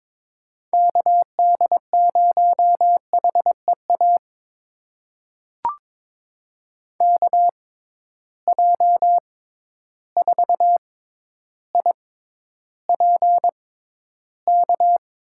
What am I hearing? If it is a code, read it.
Morse code, heard as a keyed tone, about 21 wpm, spaced as for 11 wpm, 700 Hz. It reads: KD0HEA KJ4IPK
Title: USA Call Signs - 6 Characters - 6x Character Spacing - 22wpm